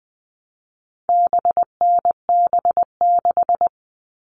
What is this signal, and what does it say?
Morse code, heard as a keyed tone, about 20 wpm, 700 Hz. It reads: BNB6
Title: All Cut Numbers - 4-Digit 20wpm